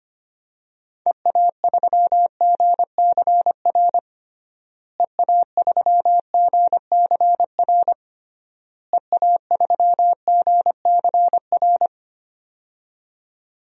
Morse code, heard as a keyed tone, about 25 wpm, 690 Hz. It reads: EA3GCR EA3GCR EA3GCR
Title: DXCC-100 Call Signs - 6 Characters - Repeated 3x - 25wpm